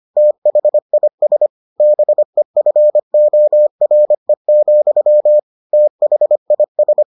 Morse code, heard as a keyed tone, about 25 wpm, 600 Hz. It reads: THIS BEFORE, THIS